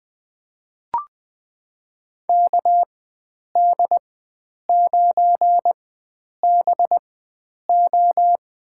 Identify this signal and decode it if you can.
Morse code, heard as a keyed tone, about 21 wpm, spaced as for 9 wpm, 700 Hz. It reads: KD9BO
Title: USA Call Signs - 4x Character Spacing - 20wpm